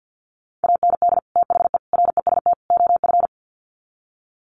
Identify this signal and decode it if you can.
Morse code, heard as a keyed tone, about 50 wpm, 700 Hz. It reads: AND THE REST OF
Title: Sets of 4 Words 50wpm